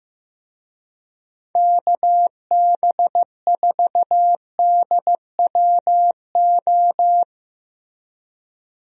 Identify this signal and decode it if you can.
Morse code, heard as a keyed tone, about 15 wpm, 700 Hz. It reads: KB4DWO
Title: USA Call Signs - 6 Characters - Repeated 2x - 15wpm